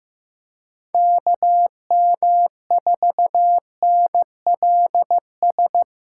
Morse code, heard as a keyed tone, about 16 wpm, 700 Hz. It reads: KM4NLS
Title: USA Call Signs - 6 Characters - 2x Character Spacing - 15wpm